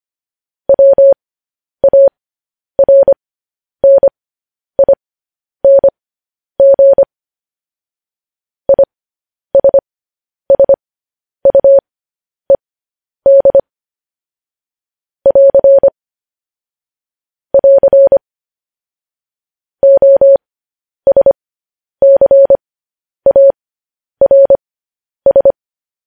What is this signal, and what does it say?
Morse code, heard as a keyed tone, about 25 wpm, spaced as for 10 wpm, 550 Hz. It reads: WARNING ISSUED <AR> <AR> OSCARS